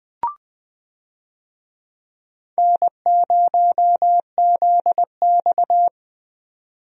Morse code, heard as a keyed tone, about 21 wpm, 700 Hz. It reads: N0ZX
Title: USA Call Signs - 4 Characters - 20wpm